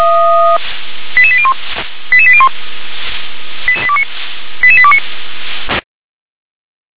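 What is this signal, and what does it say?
Five-tone selective call (ZVEI), heard as quick runs of five tones, tones of about 70 ms. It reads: 88801 88801 88829
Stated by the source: ZVEI five-tone selective call received by SDR.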